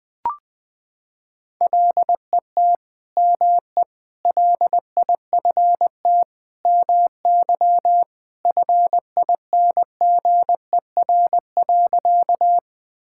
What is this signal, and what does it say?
Morse code, heard as a keyed tone, about 20 wpm, 700 Hz. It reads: LET ME LIFT MY FINGER.
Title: Sentences from Top 1600 Words 20wpm